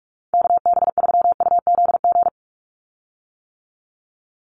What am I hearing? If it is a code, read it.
Morse code, heard as a keyed tone, about 44 wpm, 700 Hz. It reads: KB3UZG
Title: USA Call Signs 45wpm